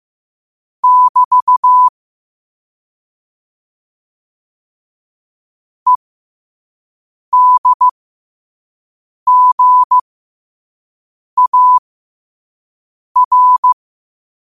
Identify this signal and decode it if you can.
Morse code, heard as a keyed tone, about 15 wpm, spaced as for 5 wpm, 1000 Hz. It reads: <BT> EDGAR